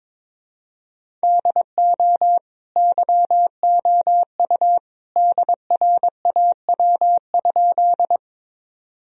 Morse code, heard as a keyed tone, about 22 wpm, 700 Hz. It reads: DO YOU DRAW?
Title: Sentences from Top 1000 Words - Review Part 1 - 22wpm